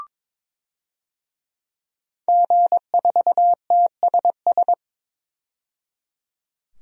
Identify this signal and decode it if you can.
Morse code, heard as a keyed tone, about 22 wpm, 700 Hz. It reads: G4TSS
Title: DXCC-100 Call Signs - 5 Characters - 22wpm